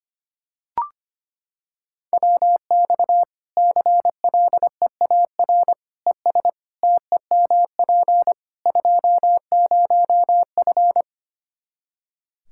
Morse code, heard as a keyed tone, about 25 wpm, 700 Hz. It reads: WX CLEAR ES TEMP 20F
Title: K1USN SST QSO Practice 25wpm